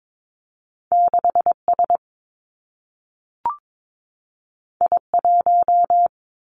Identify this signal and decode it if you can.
Morse code, heard as a keyed tone, about 22 wpm, 700 Hz. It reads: 6 S I 1
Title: Two Challenging Letter-Number Combinations 22wpm